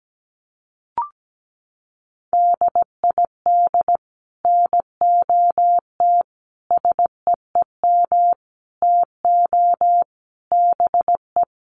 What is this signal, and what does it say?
Morse code, heard as a keyed tone, about 17 wpm, 700 Hz. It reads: DID NOT SEEM TO BE
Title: Sets of 5 Words Encore 17wpm